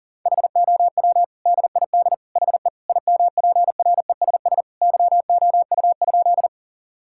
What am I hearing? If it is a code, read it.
Morse code, heard as a keyed tone, about 40 wpm, 700 Hz. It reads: HOW DID HE IMPRESS YOU?
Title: Sentences from Top 1600 Words 40wpm